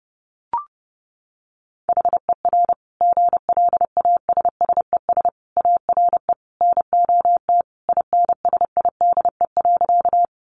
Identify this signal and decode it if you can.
Morse code, heard as a keyed tone, about 30 wpm, 700 Hz. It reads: HER GLASSES ARE NOT INSIDE.
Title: Additional Easy Phrases 30wpm